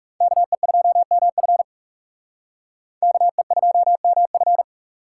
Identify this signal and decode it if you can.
Morse code, heard as a keyed tone, about 36 wpm, 700 Hz. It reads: KE2MF KE2MF
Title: Speed Racing - USA 2x2 Call Signs 30wpm